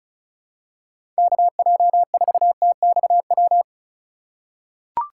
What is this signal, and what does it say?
Morse code, heard as a keyed tone, about 36 wpm, 700 Hz. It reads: KJ4TXW
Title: USA Call Signs - 4x Character Spacing - 35wpm